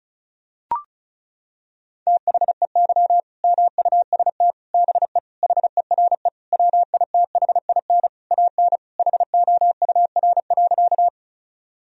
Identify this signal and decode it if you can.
Morse code, heard as a keyed tone, about 34 wpm, 700 Hz. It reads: THEY MUST BE HERE WITHIN AN HOUR.
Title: Sentences from Top 1000 Words - Review Part 4 - 35wpm